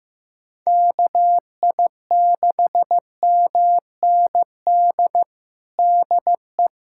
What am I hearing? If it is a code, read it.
Morse code, heard as a keyed tone, about 16 wpm, 700 Hz. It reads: KI6MND DE
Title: USA Callsign DE Callsign 15wpm